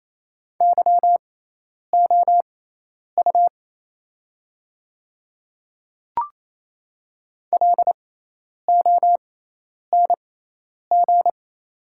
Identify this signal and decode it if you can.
Morse code, heard as a keyed tone, about 28 wpm, spaced as for 10 wpm, 690 Hz. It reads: YOU LONG
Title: 6x Character Spacing - Top 200 Words - 28wpm